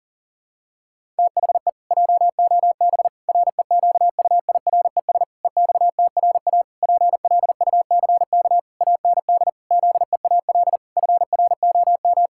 Morse code, heard as a keyed tone, about 40 wpm, 700 Hz. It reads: THE JOB REQUIRES EXTRA PLUCK AND ZEAL FROM
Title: Pangram Challenge v2 - 40wpm